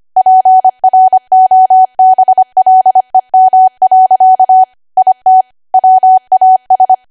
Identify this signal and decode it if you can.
Morse code, heard as a keyed tone, about 25 wpm, 750 Hz. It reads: PROBLEM. IT WAS
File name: Archive / 25 WPM / 25WPM